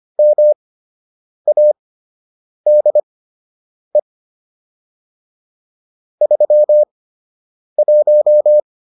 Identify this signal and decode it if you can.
Morse code, heard as a keyed tone, about 25 wpm, spaced as for 8 wpm, 600 Hz. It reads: MADE 31